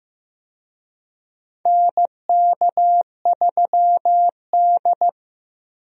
Morse code, heard as a keyed tone, about 15 wpm, 700 Hz. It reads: NK3D
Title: USA Call Signs - 4 Characters - Repeated 2x - 15wpm